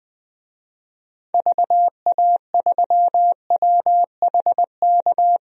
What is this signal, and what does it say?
Morse code, heard as a keyed tone, about 20 wpm, 700 Hz. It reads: VA3WHK
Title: DXCC-100 Call Signs - 6 Characters - Repeated 2x - 20wpm